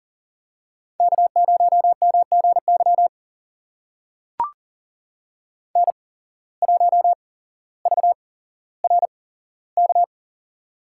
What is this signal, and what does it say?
Morse code, heard as a keyed tone, about 42 wpm, spaced as for 17 wpm, 700 Hz. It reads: K0MGY N1VRK
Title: USA Call Signs - 5 Characters - 8x Character Spacing - 40wpm